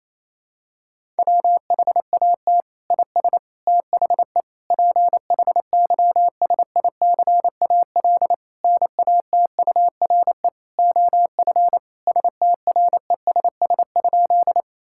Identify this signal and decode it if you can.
Morse code, heard as a keyed tone, about 27 wpm, 700 Hz. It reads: WHAT IS THE PHYSICAL NATURE OF STRESS?
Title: Sentences from Top 1000 Words - Review Part 6 - 28wpm